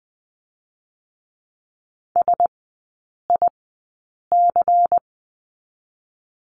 Morse code, heard as a keyed tone, about 20 wpm, 700 Hz. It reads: S I C
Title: 107 Lesson 18e - Letter m - Three Characters 20wpm